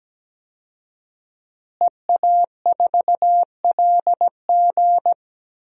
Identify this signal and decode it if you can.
Morse code, heard as a keyed tone, about 17 wpm, 700 Hz. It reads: EA4LG
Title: DXCC-100 Call Signs - 5 Characters - Repeated 3x - 17wpm